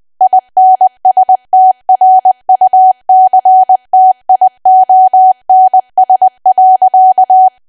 Morse code, heard as a keyed tone, about 20 wpm, 750 Hz. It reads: INSTRUCTIONS.
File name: Archive / 20 WPM / 20WPM